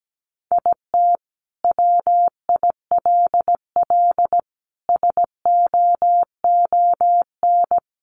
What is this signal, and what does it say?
Morse code, heard as a keyed tone, about 17 wpm, 700 Hz. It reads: IT WILL SOON
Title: Sentences from Top 1000 Words 17wpm